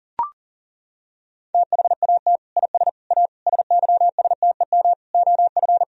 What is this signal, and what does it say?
Morse code, heard as a keyed tone, about 40 wpm, 700 Hz. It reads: THAT IS A SYSTEM OF
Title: Sentences from Top 1000 Words - Review Part 3 - 40wpm